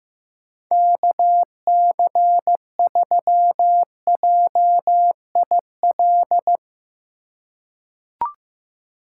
Morse code, heard as a keyed tone, about 15 wpm, 700 Hz. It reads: KC3JIL
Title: USA Call Signs - 6 Characters - 4x Character Spacing - 15wpm